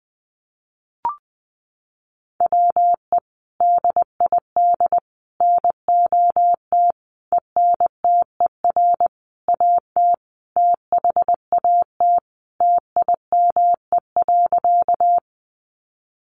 Morse code, heard as a keyed tone, about 20 wpm, 700 Hz. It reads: WE DID NOT ENTER AT THAT TIME.
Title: Sentences from Top 1300 Words 20wpm